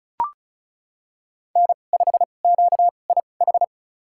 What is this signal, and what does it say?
Morse code, heard as a keyed tone, about 34 wpm, spaced as for 25 wpm, 700 Hz. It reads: N5QIH
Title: USA Call Signs - 5 Characters - 2x Character Spacing - 35wpm